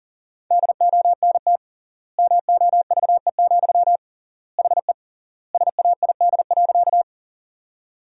Morse code, heard as a keyed tone, about 40 wpm, 700 Hz. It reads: DONT MOVE, HE SAID.
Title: 3x Word-Spacing - Sentences from Top 300 Words - 40wpm